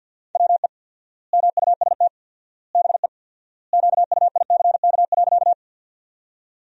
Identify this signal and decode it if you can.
Morse code, heard as a keyed tone, about 50 wpm, 700 Hz. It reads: WE MUST BE QUICK.
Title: 4x Word-Spacing - Sentences from Top 500 Words - 50wpm